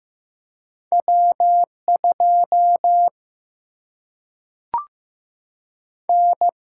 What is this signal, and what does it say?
Morse code, heard as a keyed tone, about 15 wpm, 700 Hz. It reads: W 2 N
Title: Single-Letter Single-Number 15wpm